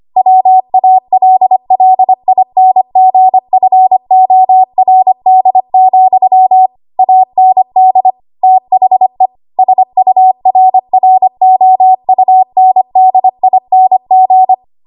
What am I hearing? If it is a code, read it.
Morse code, heard as a keyed tone, about 25 wpm, 750 Hz. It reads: WALLINGFORD, AND THE SURROUNDING
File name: Morse / 25WPM